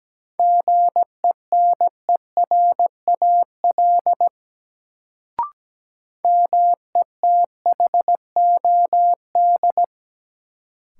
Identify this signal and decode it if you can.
Morse code, heard as a keyed tone, about 17 wpm, 700 Hz. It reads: GENERAL METHOD
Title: Sentences from Top 600 Words - New Word Drill - 17wpm